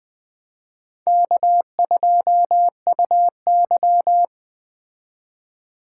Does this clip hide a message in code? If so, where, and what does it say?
Yes: Morse code from 1.1 to 4.2 s, heard as a keyed tone, about 21 wpm, 700 Hz. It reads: K2UY